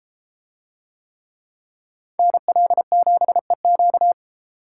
Morse code, heard as a keyed tone, about 33 wpm, 700 Hz. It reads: NL7EQ